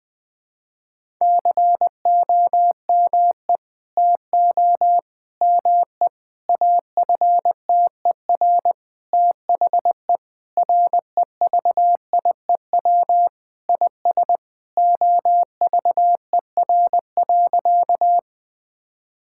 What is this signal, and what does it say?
Morse code, heard as a keyed tone, about 21 wpm, 700 Hz. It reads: COME TO ME AFTER THE REVIEW IS OVER.
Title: Sentences from Top 1000 Words - Review Part 6 - 20wpm